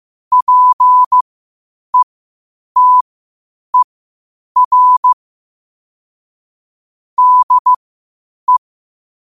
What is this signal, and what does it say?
Morse code, heard as a keyed tone, about 15 wpm, spaced as for 8 wpm, 1000 Hz. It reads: PETER DE
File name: qotdfiles / 8wpm